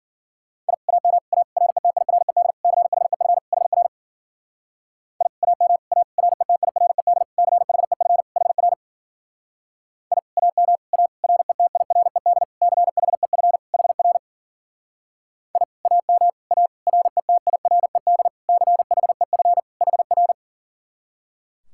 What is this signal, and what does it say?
Morse code, heard as a keyed tone, about 52 wpm, 700 Hz. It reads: I AM A RETIRED CHEF HR I AM A RETIRED CHEF HR I AM A RETIRED CHEF HR I AM A RETIRED CHEF HR